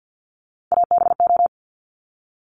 Morse code, heard as a keyed tone, about 50 wpm, 690 Hz. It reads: ABO
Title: Top Three Letter Combinations in Words 50wpm